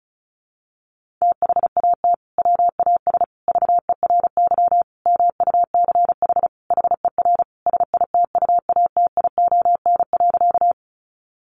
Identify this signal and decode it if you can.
Morse code, heard as a keyed tone, about 34 wpm, 700 Hz. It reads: THAT WAS VERY MUCH HER SITUATION.
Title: Sentences from Top 1200 Words 35wpm